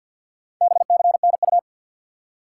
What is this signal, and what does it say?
Morse code, heard as a keyed tone, about 53 wpm, 700 Hz. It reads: BCNU
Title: Top 180 QSO Elements 50wpm